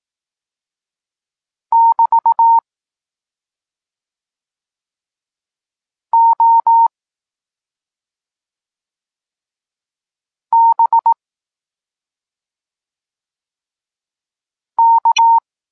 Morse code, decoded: <BT> O B K